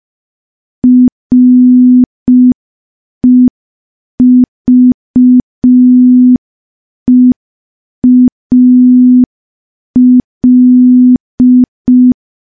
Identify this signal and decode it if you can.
Morse code, heard as a keyed tone, about 5 wpm, 250 Hz. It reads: REVEAL
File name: morse / morse-training-05WPM